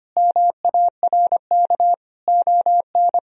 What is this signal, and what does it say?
Morse code, heard as a keyed tone, about 25 wpm, 700 Hz. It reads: MARK ON
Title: Sentences from Top 1000 Words - Review Part 1 - 25wpm